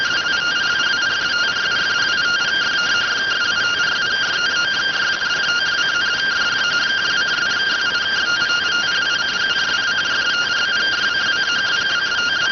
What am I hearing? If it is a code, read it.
Two-tone teleprinter signal, about 170 Hz shift.